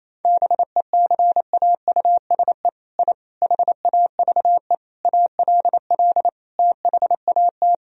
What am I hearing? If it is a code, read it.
Morse code, heard as a keyed tone, about 28 wpm, 700 Hz. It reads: BECAUSE I HAVE ALL THAT
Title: Sentences from Top 1000 Words - Review Part 4 - 28wpm